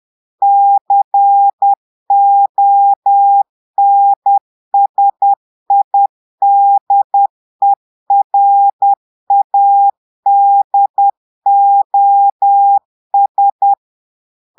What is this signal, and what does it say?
Morse code, heard as a keyed tone, about 10 wpm, 800 Hz. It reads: CONSIDERADOS
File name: portugais texte en clair 2 vitesse 10 mots minutes